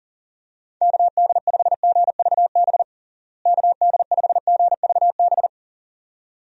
Morse code, heard as a keyed tone, about 40 wpm, 700 Hz. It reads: KD5GVB KD5GVB
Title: USA Call Signs - 6 Characters - Repeated 2x - 40wpm